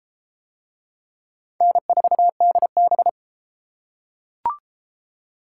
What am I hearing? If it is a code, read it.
Morse code, heard as a keyed tone, about 32 wpm, 700 Hz. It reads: N4DB